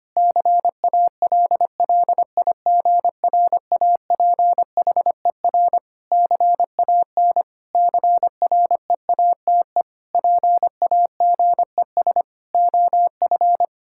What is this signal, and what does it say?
Morse code, heard as a keyed tone, about 25 wpm, 700 Hz. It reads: CALLIGRAPHER CAN CREATE PAGES OF
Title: Pangram Challenge v2 - 25wpm